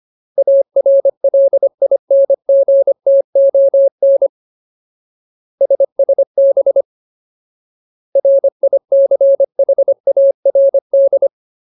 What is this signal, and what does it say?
Morse code, heard as a keyed tone, about 25 wpm, 550 Hz. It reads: ARLINGTON SSB RICHARD
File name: Ultimate Word List / 25WPM Ultimate